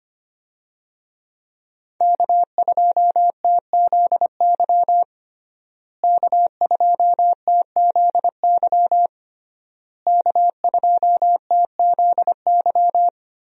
Morse code, heard as a keyed tone, about 25 wpm, 700 Hz. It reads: K2TZY K2TZY K2TZY
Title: USA Call Signs - 5 Characters - Repeated 3x - 25wpm